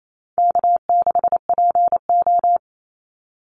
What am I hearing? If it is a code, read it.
Morse code, heard as a keyed tone, about 28 wpm, 700 Hz. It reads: K6PO
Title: Call Signs Repeated 3x - 28wpm